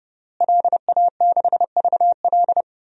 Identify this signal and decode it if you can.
Morse code, heard as a keyed tone, about 30 wpm, 700 Hz. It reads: LA6VL
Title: DXCC-100 Call Signs - 5 Characters - Repeated 3x - 30wpm